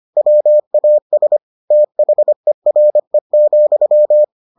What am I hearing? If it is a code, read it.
Morse code, heard as a keyed tone, about 25 wpm, 600 Hz. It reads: WAS THERE,